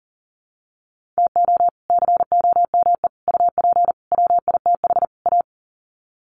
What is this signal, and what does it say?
Morse code, heard as a keyed tone, about 40 wpm, 700 Hz. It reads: TO COME UP WITH A
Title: Sets of 5 Words 40wpm